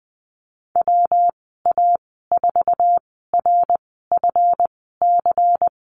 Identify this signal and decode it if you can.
Morse code, heard as a keyed tone, about 20 wpm, spaced as for 15 wpm, 700 Hz. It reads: WA4RFC
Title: 140 Lesson 23g - Letter f - Call Signs 20wpm